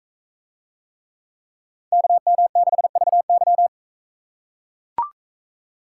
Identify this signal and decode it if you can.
Morse code, heard as a keyed tone, about 40 wpm, 700 Hz. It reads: KM6VY